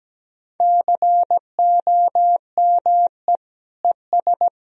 Morse code, heard as a keyed tone, about 17 wpm, 700 Hz. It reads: COME ES
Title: Sentences from Top 200 Words and Abbreviations 17wpm